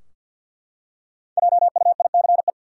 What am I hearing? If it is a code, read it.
Morse code, heard as a keyed tone, about 50 wpm, 700 Hz. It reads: JUICE
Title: Five Letter Words - 50wpm